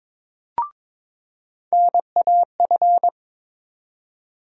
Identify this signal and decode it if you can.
Morse code, heard as a keyed tone, about 22 wpm, 700 Hz. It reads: NAF
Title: USA Call Sign Postfixes - 3 Letters 22wpm